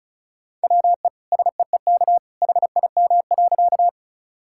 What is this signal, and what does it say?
Morse code, heard as a keyed tone, about 36 wpm, 700 Hz. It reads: WE SEEK HIM.
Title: Sentences from Top 1600 Words 35wpm